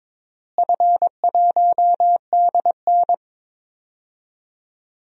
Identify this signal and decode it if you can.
Morse code, heard as a keyed tone, about 22 wpm, 700 Hz. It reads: F1DN
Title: DXCC-100 Call Signs - 4 Characters - Repeated 3x - 22wpm